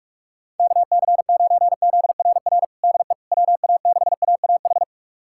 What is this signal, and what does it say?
Morse code, heard as a keyed tone, about 46 wpm, 700 Hz. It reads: KC9ZRR DE WA6AAH
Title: USA Callsign DE Callsign 45wpm